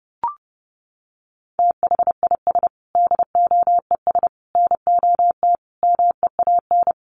Morse code, heard as a keyed tone, about 30 wpm, 700 Hz. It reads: THIS DOES NOT MEAN
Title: Sets of 5 Words Encore2 30wpm